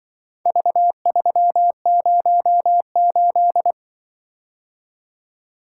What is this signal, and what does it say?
Morse code, heard as a keyed tone, about 24 wpm, 700 Hz. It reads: V308
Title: Speed-Racing - Four Letter-Number Combinations - 20wpm